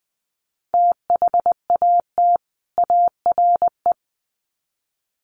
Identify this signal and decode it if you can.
Morse code, heard as a keyed tone, about 20 wpm, 700 Hz. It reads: THAT ARE
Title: Sets of 2 Words 20wpm